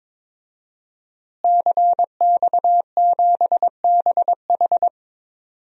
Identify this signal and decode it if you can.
Morse code, heard as a keyed tone, about 22 wpm, 700 Hz. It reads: CX7BH
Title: DXCC-100 Call Signs - 5 Characters - Repeated 2x - 22wpm